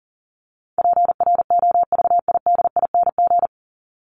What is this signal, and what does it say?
Morse code, heard as a keyed tone, about 40 wpm, 700 Hz. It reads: PROVIDING
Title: ING Suffix - Top 100 Words with Base Words 40wpm